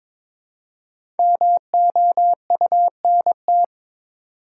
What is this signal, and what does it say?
Morse code, heard as a keyed tone, about 22 wpm, 690 Hz. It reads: MOUNT
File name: 5 Letter Words - First 3 Letters Given - 22wpm